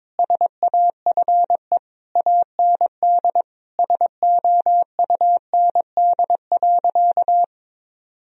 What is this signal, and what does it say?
Morse code, heard as a keyed tone, about 22 wpm, 700 Hz. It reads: SAFE AND SOUND.
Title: Sentences from Top 1000 Words - Review Part 1 - 22wpm